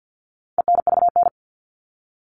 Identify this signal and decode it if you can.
Morse code, heard as a keyed tone, about 50 wpm, 700 Hz. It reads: ENVN
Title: All Cut Numbers - 4-Digit 50wpm